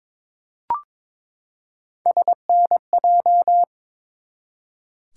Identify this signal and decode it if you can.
Morse code, heard as a keyed tone, about 22 wpm, 700 Hz. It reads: SNJ
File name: ARRL Sections v2 - 22wpm